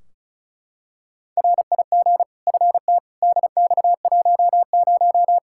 Morse code, heard as a keyed tone, about 36 wpm, 700 Hz. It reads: RIG FT DX10
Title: Practice QSO - Exchange 2 - Info - 35wpm